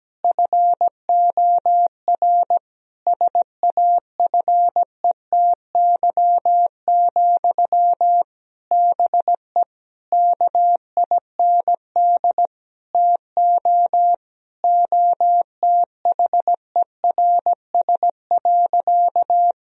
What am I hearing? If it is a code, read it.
Morse code, heard as a keyed tone, about 17 wpm, 700 Hz. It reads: FOR SAFETY, BE KIND TO OTHERS.
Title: Sentences from Top 1000 Words - Review Part 4 - 17wpm